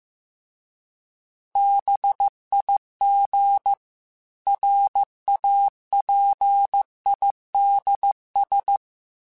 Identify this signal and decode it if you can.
Morse code, heard as a keyed tone, about 15 wpm, 800 Hz. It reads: BIG RAPIDS
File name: fromSMs / FromSMs mp3s / Cities@13